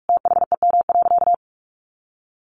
Morse code, heard as a keyed tone, about 44 wpm, 700 Hz. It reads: THEM.
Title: Sentences from Top 1400 Words 45wpm